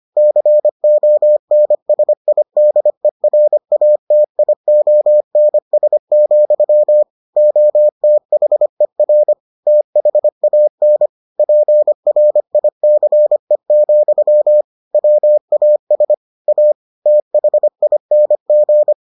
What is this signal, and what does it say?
Morse code, heard as a keyed tone, about 25 wpm, 600 Hz. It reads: CONSIDERATIONS, OTHER THAN PRICE, WAS A THING